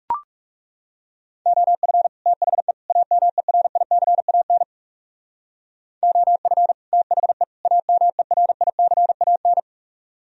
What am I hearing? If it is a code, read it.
Morse code, heard as a keyed tone, about 37 wpm, 700 Hz. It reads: OF THE AMERICAN OF THE AMERICAN
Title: Speed-Racing - Sets of 3 Words 30wpm